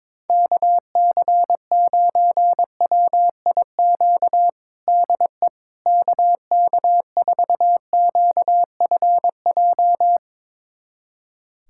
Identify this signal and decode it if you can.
Morse code, heard as a keyed tone, about 22 wpm, 700 Hz. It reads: KC9WIQ DE KK4QFJ
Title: USA Callsign DE Callsign - 6 Characters - 22wpm